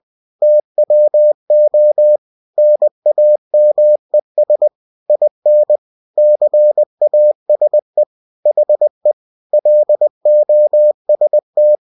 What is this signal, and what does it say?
Morse code, heard as a keyed tone, about 20 wpm, 600 Hz. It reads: TWO NAMES IN CASE HE LOST
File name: cw-books / winnie-the-pooh / winnie-20wpm-page-0020